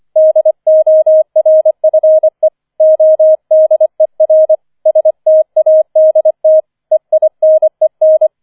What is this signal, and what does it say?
Morse code, heard as a keyed tone, about 25 wpm, 630 Hz. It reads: DORFE ODER STADT EINEN
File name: 005 Fünfter Streich 25wpm